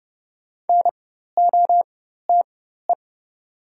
Morse code, heard as a keyed tone, about 30 wpm, spaced as for 14 wpm, 700 Hz. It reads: NOTE
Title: 4x Character Spacing - Top 500 Words - 30wpm